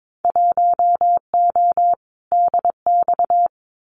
Morse code, heard as a keyed tone, about 22 wpm, 700 Hz. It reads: 1O DX
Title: Winter Field Day Exchanges 22wpm